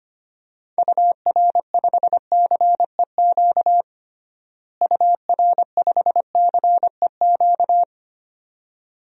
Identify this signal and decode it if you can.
Morse code, heard as a keyed tone, about 25 wpm, 700 Hz. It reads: UR5CEQ UR5CEQ
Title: DXCC-100 Call Signs - 6 Characters - Repeated 2x - 25wpm